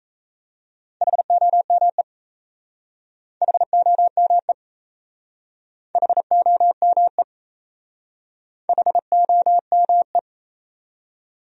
Morse code, heard as a keyed tone, about 37 wpm, 700 Hz. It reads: HOME HOME HOME HOME